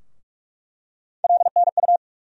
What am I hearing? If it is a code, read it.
Morse code, heard as a keyed tone, about 46 wpm, 700 Hz. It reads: LNU
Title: USA Call Sign Postfixes - 3 Letters 45wpm